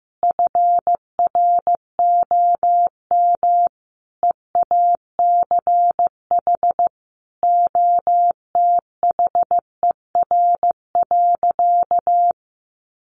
Morse code, heard as a keyed tone, about 14 wpm, 700 Hz. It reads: FROM EACH OTHER.
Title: Easy Phrases 15wpm